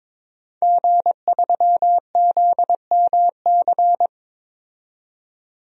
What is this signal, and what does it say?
Morse code, heard as a keyed tone, about 22 wpm, 700 Hz. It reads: G3ZMC
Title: DXCC-100 Call Signs - 5 Characters - 22wpm